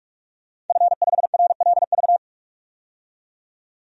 Morse code, heard as a keyed tone, about 41 wpm, 700 Hz. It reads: F5RLV
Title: DXCC-100 Call Signs - 5 Characters - 45wpm